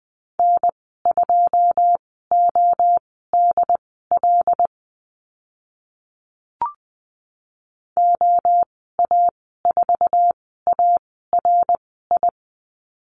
Morse code, heard as a keyed tone, about 20 wpm, spaced as for 15 wpm, 700 Hz. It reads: N2ODL OA4ARI
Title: 082 Lesson 14g - Number 2 - Call Signs 20wpm